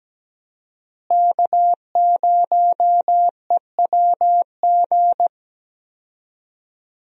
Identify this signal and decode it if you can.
Morse code, heard as a keyed tone, about 17 wpm, 700 Hz. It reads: K0EWG
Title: USA Call Signs - 5 Characters - Repeated 3x - 17wpm